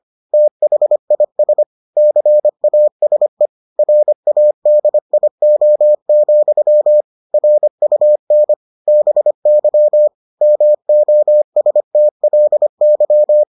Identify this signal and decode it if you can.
Morse code, heard as a keyed tone, about 25 wpm, 600 Hz. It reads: THIS CASE RADIO, RUN BY MOSTLY